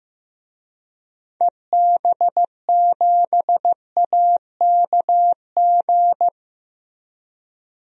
Morse code, heard as a keyed tone, about 15 wpm, 700 Hz. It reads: EB7AKG
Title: DXCC-100 Call Signs - 6 Characters - Repeated 3x - 15wpm